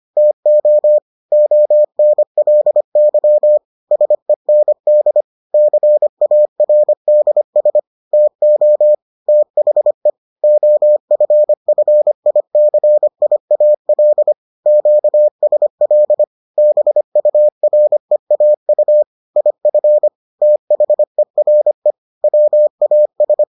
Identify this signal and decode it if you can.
Morse code, heard as a keyed tone, about 25 wpm, 600 Hz. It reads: TO ONLY SEND CARDS TO THE OFFICIAL QSL BUREAU IF THERE WAS